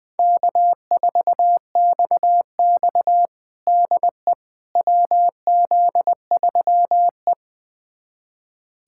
Morse code, heard as a keyed tone, about 20 wpm, 700 Hz. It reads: K4XX DE WZ3E
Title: USA Callsign DE Callsign - 4 Characters - 20wpm